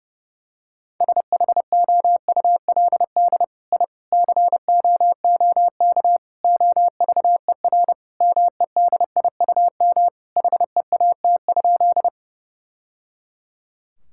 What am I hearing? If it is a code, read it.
Morse code, heard as a keyed tone, about 30 wpm, 700 Hz. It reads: SHOULD I COOK OVER MEDIUM HEAT?
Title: Sentences from Top 1000 Words - Review Part 6 - 30wpm